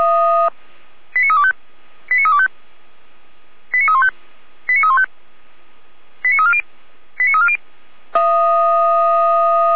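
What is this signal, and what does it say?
Five-tone selective call (ZVEI), heard as quick runs of five tones, tones of about 70 ms. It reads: 89325 89325 89315 89315 89340 89340
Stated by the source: ZVEI five-tone selective call received via SDR.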